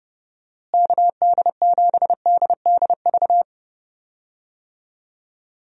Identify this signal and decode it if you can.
Morse code, heard as a keyed tone, about 30 wpm, 700 Hz. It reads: KD7DDV
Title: USA Call Signs - 6 Characters - Repeated 2x - 30wpm